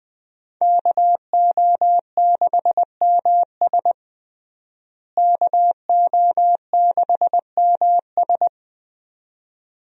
Morse code, heard as a keyed tone, about 21 wpm, 700 Hz. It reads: KO6MS KO6MS
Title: USA Call Signs - 5 Characters - Repeated 2x - 20wpm